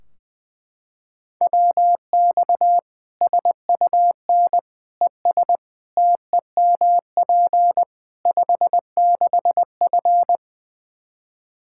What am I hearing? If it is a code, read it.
Morse code, heard as a keyed tone, about 20 wpm, 700 Hz. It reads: WX SUN ES TEMP 56F
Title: Practice QSO - Exchange 2 - Info - 20wpm